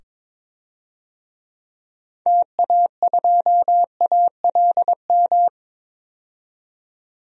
Morse code, heard as a keyed tone, about 22 wpm, 700 Hz. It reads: TA2ALM
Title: DXCC-100 Call Signs - 6 Characters - 22wpm